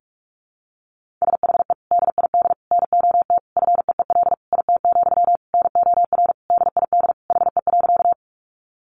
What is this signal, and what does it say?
Morse code, heard as a keyed tone, about 44 wpm, 700 Hz. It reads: SHE DID NOT FEEL IT, NOR DID HE.
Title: Sentences from Top 1500 Words 45wpm